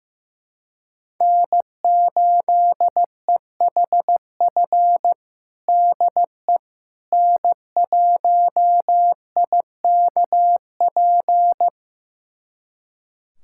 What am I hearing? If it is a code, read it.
Morse code, heard as a keyed tone, about 15 wpm, 700 Hz. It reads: N8EHF DE N1IKP
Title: USA Callsign DE Callsign - 5 Characters - 15wpm